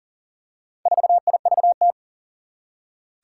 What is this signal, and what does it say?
Morse code, heard as a keyed tone, about 41 wpm, 700 Hz. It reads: 4IVT
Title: Four Challenging Letter-Number Combinations 40wpm